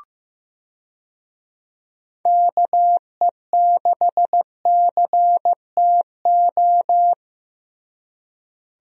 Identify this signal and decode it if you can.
Morse code, heard as a keyed tone, about 15 wpm, 700 Hz. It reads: KE6CTO
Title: USA Call Signs - 6 Characters - Repeated 2x - 15wpm